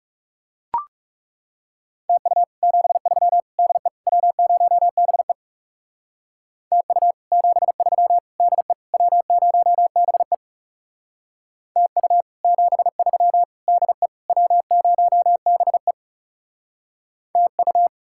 Morse code, heard as a keyed tone, about 39 wpm, 700 Hz. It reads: TU 73 DE W0BE TU 73 DE W0BE TU 73 DE W0BE TU